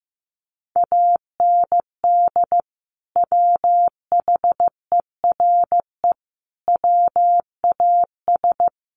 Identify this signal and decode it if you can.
Morse code, heard as a keyed tone, about 15 wpm, 700 Hz. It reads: AND WHERE WAS
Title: Sentences from Top 200 Words 15wpm